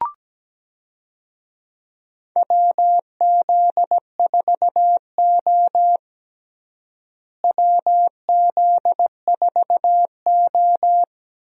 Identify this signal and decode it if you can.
Morse code, heard as a keyed tone, about 17 wpm, 700 Hz. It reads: WZ4O WZ4O
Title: USA Call Signs - 4 Characters - Repeated 2x - 17wpm